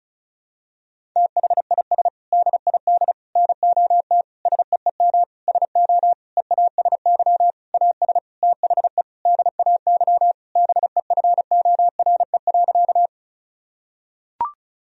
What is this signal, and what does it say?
Morse code, heard as a keyed tone, about 35 wpm, 700 Hz. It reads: THIS DID NOT SEEM SO EASY AS THE DAY BEFORE.
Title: Sentences from Top 1000 Words - Review Part 1 - 35wpm